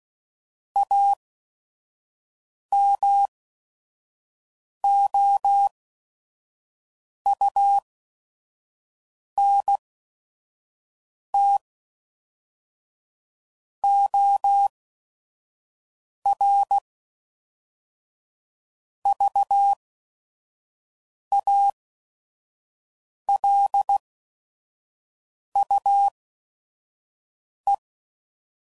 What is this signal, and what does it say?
Morse code, heard as a keyed tone, about 16 wpm, spaced as for 5 wpm, 800 Hz. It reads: AMOUNT OR VALUE
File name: CW Code Groups @5WPM / Technical Subjects / XC